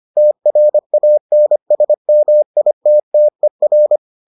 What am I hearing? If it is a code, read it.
Morse code, heard as a keyed tone, about 25 wpm, 600 Hz. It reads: TRANSMITTER